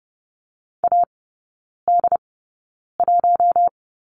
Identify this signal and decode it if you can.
Morse code, heard as a keyed tone, about 30 wpm, spaced as for 9 wpm, 700 Hz. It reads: AD1
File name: Lesson 15e - Number 5 - Three Characters 30wpm